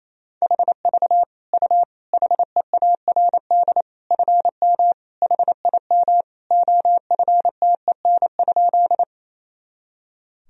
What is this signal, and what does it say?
Morse code, heard as a keyed tone, about 27 wpm, 700 Hz. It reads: HV U HEARD FM HIM OFTEN?
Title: Sentences from Top 400 Words and Abbreviations 28wpm